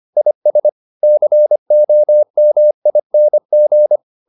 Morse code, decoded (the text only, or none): IS COMING